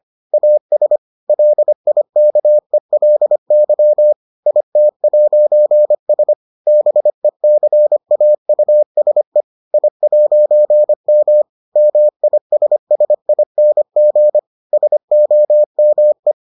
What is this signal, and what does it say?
Morse code, heard as a keyed tone, about 25 wpm, 600 Hz. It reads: AS LIKELY IT'S BECAUSE I'M MISSING SOME